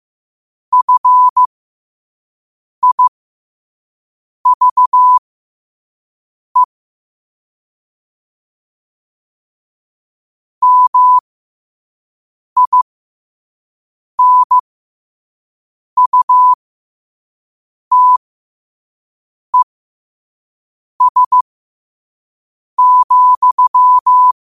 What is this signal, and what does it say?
Morse code, heard as a keyed tone, about 15 wpm, spaced as for 5 wpm, 1000 Hz. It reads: FIVE MINUTES,
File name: qotdfiles / 5wpm